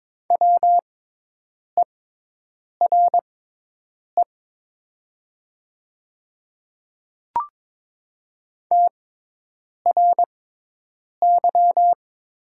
Morse code, heard as a keyed tone, about 22 wpm, spaced as for 8 wpm, 700 Hz. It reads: WERE TRY
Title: 6x Character Spacing - Top 200 Words - 22wpm